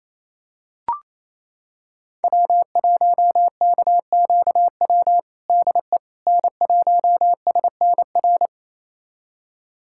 Morse code, heard as a keyed tone, about 28 wpm, 700 Hz. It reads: W1KQW DE N1SNR
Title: USA Callsign DE Callsign - 5 Characters - 28wpm